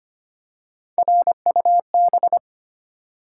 Morse code, heard as a keyed tone, about 25 wpm, 700 Hz. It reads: RUB
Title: Three Letter Words - 25wpm